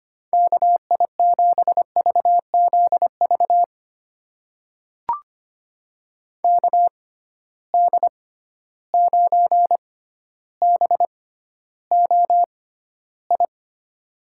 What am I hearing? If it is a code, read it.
Morse code, heard as a keyed tone, about 25 wpm, spaced as for 13 wpm, 700 Hz. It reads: KI7VZV KD9BOI
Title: USA Call Signs - 6 Characters - 6x Character Spacing - 25wpm